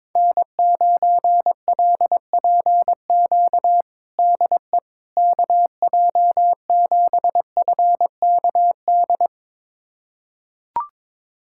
Morse code, heard as a keyed tone, about 21 wpm, 700 Hz. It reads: N9LPQ DE KJ7FKD
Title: USA Callsign DE Callsign 22wpm